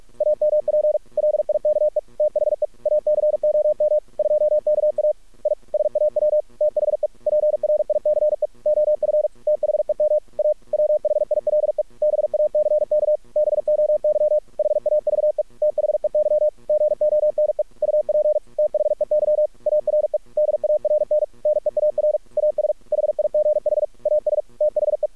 Morse code, heard as a keyed tone, about 46 wpm, 600 Hz. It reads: AMP LIKE THE ACOM 2KA I SAW THE PRICE OF THEM A WHILE BACK BOY HAVE THEY GONE UP THEY ARE DARN NEAR AS HIGH AS THE
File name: archived / CRSnet